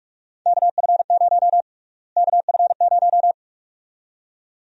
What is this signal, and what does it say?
Morse code, heard as a keyed tone, about 41 wpm, 700 Hz. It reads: KF0 KF0
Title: USA Call Sign Prefixes - Repeated 2x - 45wpm